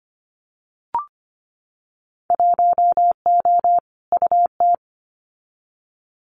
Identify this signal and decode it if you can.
Morse code, heard as a keyed tone, about 25 wpm, 700 Hz. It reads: 1O UT
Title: Winter Field Day Exchanges Encore 25wpm